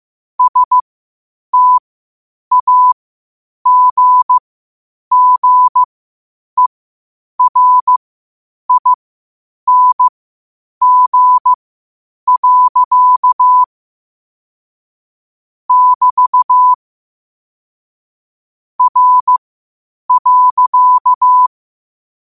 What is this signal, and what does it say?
Morse code, heard as a keyed tone, about 15 wpm, spaced as for 8 wpm, 1000 Hz. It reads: STAGGERING. <BT> R.
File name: qotdfiles / 8wpm